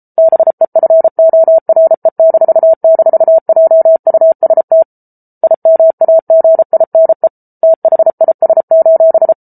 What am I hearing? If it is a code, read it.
Morse code, heard as a keyed tone, about 34 wpm, 650 Hz. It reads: BEFORE--JUST IMAGINE THIS: